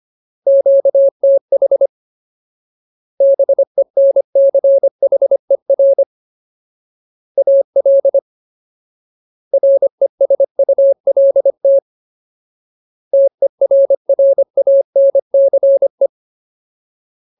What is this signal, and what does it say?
Morse code, heard as a keyed tone, about 25 wpm, 550 Hz. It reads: QTH BENCHER AL RESULT TERRANCE
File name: Ultimate Word List / 25WPM Ultimate